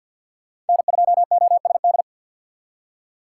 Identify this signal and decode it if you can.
Morse code, heard as a keyed tone, about 53 wpm, 700 Hz. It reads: N2OSD
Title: USA Call Signs - 5 Characters - 8x Character Spacing - 50wpm